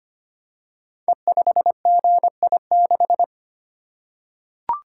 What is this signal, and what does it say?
Morse code, heard as a keyed tone, about 25 wpm, 700 Hz. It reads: E5GI6
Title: Code Groups - 5 Letter:Numbers - 25wpm